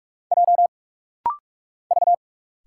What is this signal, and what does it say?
Morse code, heard as a keyed tone, about 44 wpm, 700 Hz. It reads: J V
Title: Single Letters - Rapid-Fire 45wpm